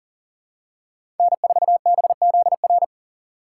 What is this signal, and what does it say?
Morse code, heard as a keyed tone, about 37 wpm, 700 Hz. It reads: N4BZR
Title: USA Call Signs - 5 Characters - 2x Character Spacing - 40wpm